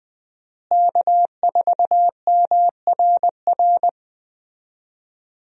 Morse code, heard as a keyed tone, about 20 wpm, 700 Hz. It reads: K4MRR
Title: USA Call Signs - 5 Characters - Repeated 3x - 20wpm